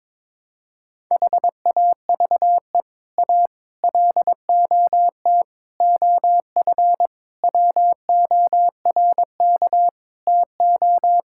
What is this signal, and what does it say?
Morse code, heard as a keyed tone, about 21 wpm, 700 Hz. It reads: HAVE A LOT OF WORK TO
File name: practice / 22wpm / Sets of 7 Words - 22wpm